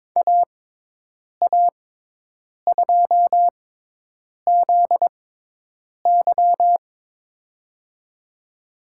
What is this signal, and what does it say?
Morse code, heard as a keyed tone, about 22 wpm, spaced as for 8 wpm, 700 Hz. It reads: AA2ZY
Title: USA Call Signs - 5 Characters - 6x Character Spacing - 22wpm